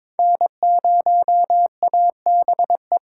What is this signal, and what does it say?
Morse code, heard as a keyed tone, about 22 wpm, 700 Hz. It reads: N0ABE
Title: USA Call Signs - 4x Character Spacing - 22wpm